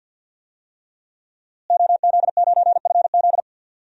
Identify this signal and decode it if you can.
Morse code, heard as a keyed tone, about 52 wpm, 700 Hz. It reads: OZ9FZ
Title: DXCC-100 Call Signs - 5 Characters - Repeated 3x - 50wpm